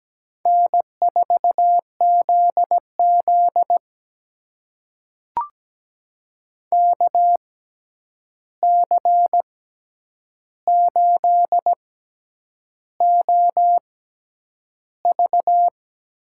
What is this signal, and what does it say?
Morse code, heard as a keyed tone, about 17 wpm, spaced as for 8 wpm, 700 Hz. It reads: N4ZZ KC8OV
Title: USA Call Signs - 6x Character Spacing - 17wpm